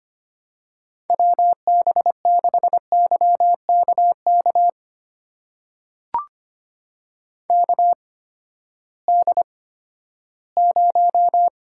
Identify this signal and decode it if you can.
Morse code, heard as a keyed tone, about 25 wpm, spaced as for 14 wpm, 700 Hz. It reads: WB6YKK KD0
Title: USA Call Signs - 6 Characters - 8x Character Spacing - 25wpm